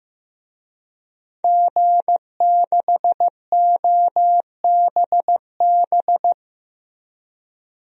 Morse code, heard as a keyed tone, about 15 wpm, 700 Hz. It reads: G6OBB
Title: DXCC-100 Call Signs - 5 Characters - Repeated 2x - 15wpm